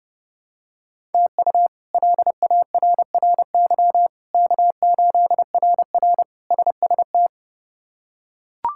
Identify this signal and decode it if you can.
Morse code, heard as a keyed tone, about 31 wpm, 700 Hz. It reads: TU LARRY K8RR SST
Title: K1USN SST QSO Practice - 30wpm